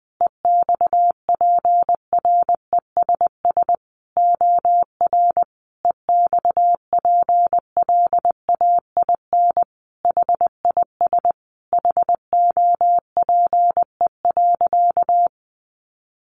Morse code, heard as a keyed tone, about 20 wpm, 700 Hz. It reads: EXPRESS OR EXPLAIN HIS HOPE.
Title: Sentences from Top 1100 Words 20wpm